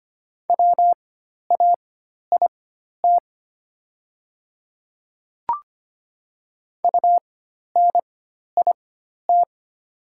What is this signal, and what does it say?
Morse code, heard as a keyed tone, about 25 wpm, spaced as for 12 wpm, 700 Hz. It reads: WAIT UNIT
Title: 4x Character Spacing - Top 500 Words - 25wpm